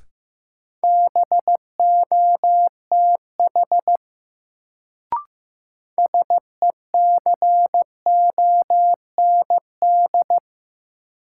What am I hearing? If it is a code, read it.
Morse code, heard as a keyed tone, about 15 wpm, 700 Hz. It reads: BOTH SECOND
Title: Sentences from Top 300 Words - New Word Drill - 15wpm